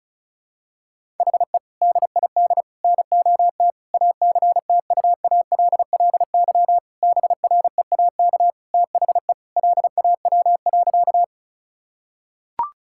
Morse code, heard as a keyed tone, about 34 wpm, 700 Hz. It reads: HE DID NOT ACTUALLY BREAK THE LAW.
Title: Sentences from Top 1000 Words - Review Part 6 - 35wpm